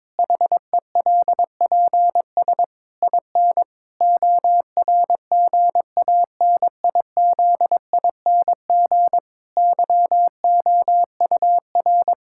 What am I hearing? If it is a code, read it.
Morse code, heard as a keyed tone, about 22 wpm, 700 Hz. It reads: HELPS IN ORGANIZING YOUR
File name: Difficult Sentences with Hints - 22wpm